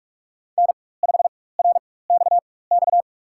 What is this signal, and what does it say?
Morse code, heard as a keyed tone, about 44 wpm, spaced as for 21 wpm, 700 Hz. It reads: N5RXX
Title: USA Call Signs - 5 Characters - 4x Character Spacing - 45wpm